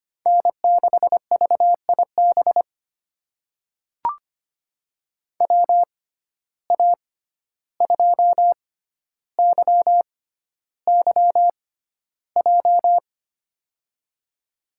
Morse code, heard as a keyed tone, about 25 wpm, spaced as for 12 wpm, 700 Hz. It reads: N6VIB WA2YYJ
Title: USA Call Signs - 6x Character Spacing - 25wpm